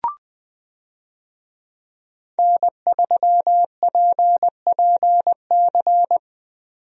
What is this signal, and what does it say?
Morse code, heard as a keyed tone, about 20 wpm, 700 Hz. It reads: N3PPC
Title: USA Call Signs - 5 Characters - 20wpm